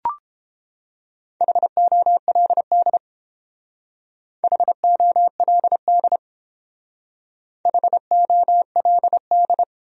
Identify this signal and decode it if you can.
Morse code, heard as a keyed tone, about 30 wpm, 700 Hz. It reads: HOLD HOLD HOLD